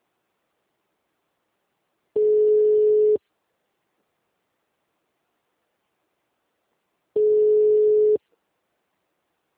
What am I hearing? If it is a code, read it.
Ringback tone (Europe, 425 Hz).